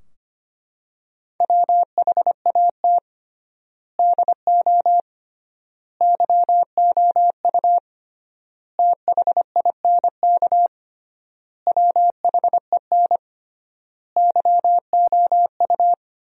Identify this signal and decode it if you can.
Morse code, heard as a keyed tone, about 25 wpm, 700 Hz. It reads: WHAT DO YOU THINK WHEN YOU
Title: 3x Word-Spacing - Sentences from Top 300 Words - 25wpm